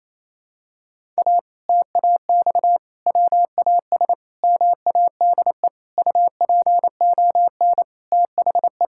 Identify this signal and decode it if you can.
Morse code, heard as a keyed tone, about 28 wpm, 700 Hz. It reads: A TAX WAS MADE UPON THE
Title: Sentences from Top 1000 Words - Review Part 6 - 28wpm